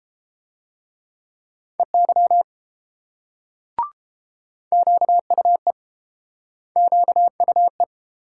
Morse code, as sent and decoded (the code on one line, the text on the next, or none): . -.-- / --.- ..- . / --.- ..- .
EY QUE QUE